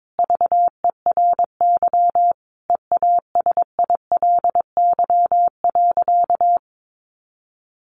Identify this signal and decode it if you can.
Morse code, heard as a keyed tone, about 22 wpm, 700 Hz. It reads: VERY EASILY.
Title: Sentences from Top 900 Words 22wpm